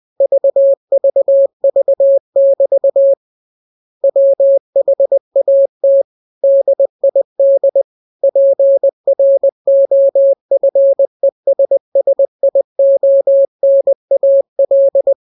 Morse code, heard as a keyed tone, about 20 wpm, 550 Hz. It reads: VVV<BT> WHAT DID PROFESSIONAL
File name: jokes / joke 2 20 0000